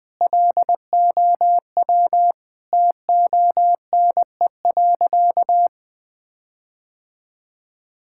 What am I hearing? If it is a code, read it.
Morse code, heard as a keyed tone, about 20 wpm, 700 Hz. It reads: LOW TONE.
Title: Sentences from Top 1600 Words 20wpm